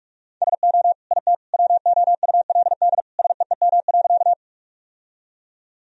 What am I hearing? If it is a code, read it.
Morse code, heard as a keyed tone, about 43 wpm, 700 Hz. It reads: SO IT WOULD SEEM.
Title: Sentences from Top 1000 Words - Review Part 1 - 45wpm